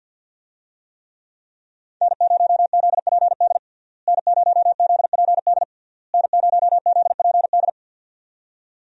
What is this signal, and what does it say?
Morse code, heard as a keyed tone, about 49 wpm, 700 Hz. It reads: N0ZPD N0ZPD N0ZPD
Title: USA Call Signs - 5 Characters - Repeated 3x - 50wpm